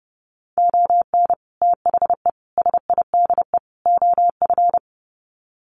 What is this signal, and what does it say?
Morse code, heard as a keyed tone, about 30 wpm, 700 Hz. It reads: ON THE SIDE OF
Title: Sets of 4 Words 30wpm